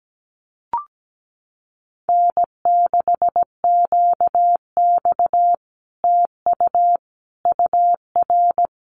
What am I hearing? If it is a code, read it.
Morse code, heard as a keyed tone, about 17 wpm, 700 Hz. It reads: N6QX TU UR
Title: Common QSO Phrases Encore 17wpm